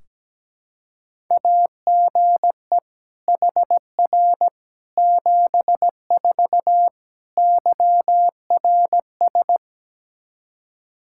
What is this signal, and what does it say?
Morse code, heard as a keyed tone, about 17 wpm, 700 Hz. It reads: AGE HR 74 YRS
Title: Practice QSO - Exchange 3 - More Info - 17wpm